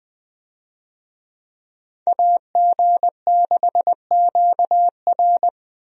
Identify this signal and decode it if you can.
Morse code, heard as a keyed tone, about 20 wpm, 700 Hz. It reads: AG6QR
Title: USA Call Signs - 5 Characters - Repeated 3x - 20wpm